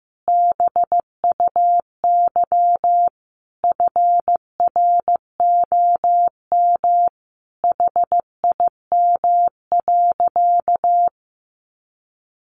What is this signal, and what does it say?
Morse code, heard as a keyed tone, about 15 wpm, 700 Hz. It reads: BUY FROM HIM.
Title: Sentences from Top 1200 Words 15wpm